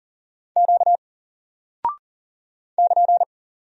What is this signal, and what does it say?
Morse code, heard as a keyed tone, about 39 wpm, 700 Hz. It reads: Q <KN>
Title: Single Character (Letter:Number:Punctuation:Prosign) 40wpm